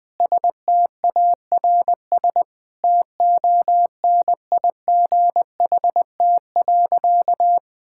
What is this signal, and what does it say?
Morse code, heard as a keyed tone, about 20 wpm, 700 Hz. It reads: STARS TONIGHT.
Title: Sentences with 3-Word Phrases - 20wpm